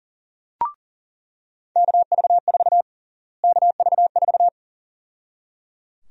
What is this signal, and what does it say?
Morse code, heard as a keyed tone, about 37 wpm, 700 Hz. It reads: KV4 KV4
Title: USA Call Sign Prefixes - Repeated 2x - 40wpm